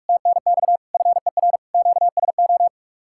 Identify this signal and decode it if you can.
Morse code, heard as a keyed tone, about 45 wpm, 700 Hz. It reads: TNX FER QSO